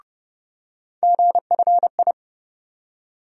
Morse code, heard as a keyed tone, about 30 wpm, 700 Hz. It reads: GFI
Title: USA Call Sign Postfixes - 3 Letters 30wpm